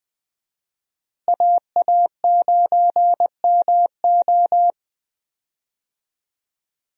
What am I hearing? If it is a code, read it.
Morse code, heard as a keyed tone, about 20 wpm, 700 Hz. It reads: AA9MO
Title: USA Call Signs - 5 Characters - Repeated 2x - 20wpm